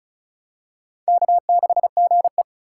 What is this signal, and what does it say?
Morse code, heard as a keyed tone, about 36 wpm, 700 Hz. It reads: K6GE
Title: USA Call Signs - 2x Character Spacing - 35wpm